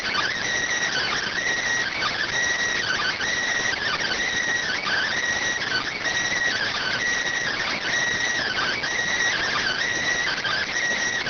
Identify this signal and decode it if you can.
Emergency Alert System (SAME) tones (header bursts), heard as harsh buzzing bursts. It is heard from 0.9 to 10.7 s.